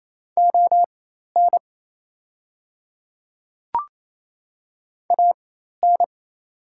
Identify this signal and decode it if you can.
Morse code, heard as a keyed tone, about 28 wpm, spaced as for 13 wpm, 700 Hz. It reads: ON AN
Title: 4x Character Spacing - Top 100 Words - 28wpm